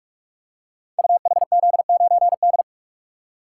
Morse code, heard as a keyed tone, about 44 wpm, 700 Hz. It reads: UHZ9D
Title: Code Groups - 5 Letter:Numbers - 45wpm